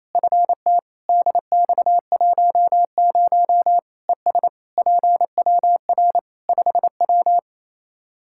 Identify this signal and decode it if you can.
Morse code, heard as a keyed tone, about 28 wpm, 700 Hz. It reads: FT DX10 ES PWR 5W
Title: Practice QSO - Exchange 2 - Info - 28wpm